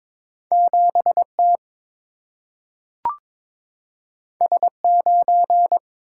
Morse code, heard as a keyed tone, about 22 wpm, 700 Hz. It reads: 7 T S 9
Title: Single-Letter Single-Number 22wpm